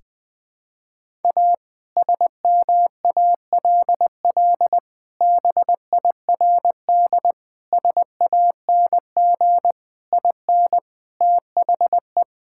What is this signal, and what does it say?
Morse code, heard as a keyed tone, about 20 wpm, 700 Hz. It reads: A SMALL BIRD SANG IN THE
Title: Sentences with 2-Word Phrases - 20wpm